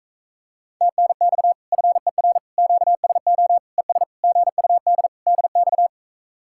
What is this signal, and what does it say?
Morse code, heard as a keyed tone, about 37 wpm, 700 Hz. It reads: TNX FER QSO ES GUD DX